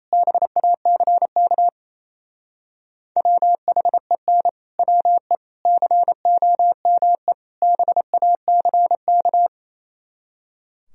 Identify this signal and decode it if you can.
Morse code, heard as a keyed tone, about 34 wpm, 700 Hz. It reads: BACK WHEN WE COME BACK
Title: Speed-Racing - Sets of 4 Words 28wpm